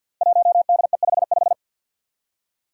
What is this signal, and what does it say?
Morse code, heard as a keyed tone, about 50 wpm, 700 Hz. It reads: 1DE55
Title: Five Challenging Letter-Number Combinations 50wpm